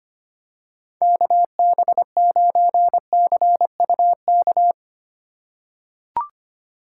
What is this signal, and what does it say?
Morse code, heard as a keyed tone, about 25 wpm, 700 Hz. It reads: KB9CUK
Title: USA Call Signs - 3x Character Spacing - 25wpm